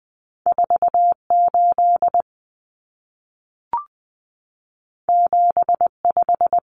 Morse code, heard as a keyed tone, about 20 wpm, 700 Hz. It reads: 48 75
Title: Common Cut Numbers - 2-Digit 20wpm